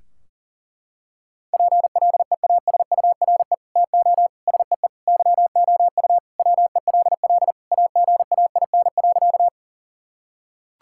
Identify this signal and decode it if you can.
Morse code, heard as a keyed tone, about 40 wpm, 700 Hz. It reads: PLEASURE TO SEE YOU WELL AGAIN.
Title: Sentences from Top 1600 Words 40wpm